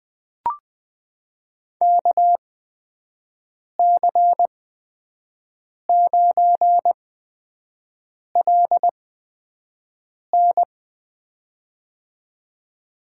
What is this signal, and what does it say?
Morse code, heard as a keyed tone, about 20 wpm, spaced as for 5 wpm, 700 Hz. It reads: KC9LN
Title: USA Call Signs - 8x Character Spacing - 20wpm